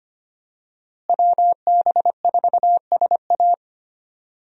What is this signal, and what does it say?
Morse code, heard as a keyed tone, about 25 wpm, 700 Hz. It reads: WB4SA
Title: USA Call Signs - 5 Characters - Repeated 2x - 25wpm